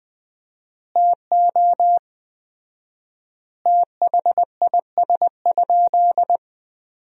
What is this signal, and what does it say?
Morse code, heard as a keyed tone, about 21 wpm, 700 Hz. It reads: TO THIS?
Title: 4x Word-Spacing - Sentences from Top 100 Words - 20wpm